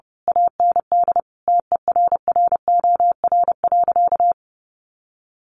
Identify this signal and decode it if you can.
Morse code, heard as a keyed tone, about 30 wpm, 700 Hz. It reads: AND TERROR.
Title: A Sound Of Thunder 30wpm